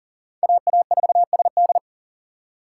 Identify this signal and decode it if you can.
Morse code, heard as a keyed tone, about 40 wpm, 700 Hz. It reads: AA4SD
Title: USA Call Signs - 5 Characters - 4x Character Spacing - 40wpm